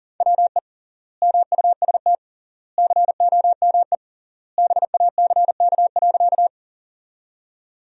Morse code, heard as a keyed tone, about 40 wpm, 700 Hz. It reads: WE MUST COME BACK.
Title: 3x Word-Spacing - Sentences from Top 200 Words - 40wpm